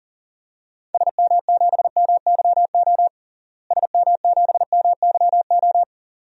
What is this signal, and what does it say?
Morse code, heard as a keyed tone, about 40 wpm, 700 Hz. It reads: SM7MYO SM7MYO
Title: DXCC-100 Call Signs - 6 Characters - Repeated 2x - 40wpm